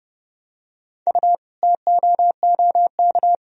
Morse code, heard as a keyed tone, about 30 wpm, 700 Hz. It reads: U TOOK
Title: Sentences from Top 300 Words and Abbreviations 30wpm